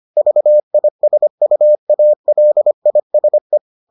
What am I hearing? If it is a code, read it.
Morse code, heard as a keyed tone, about 25 wpm, 600 Hz. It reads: VISUALISE